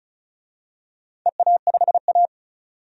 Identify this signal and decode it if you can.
Morse code, heard as a keyed tone, about 36 wpm, 700 Hz. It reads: EA5A
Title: DXCC-100 Call Signs - 4 Characters - 35wpm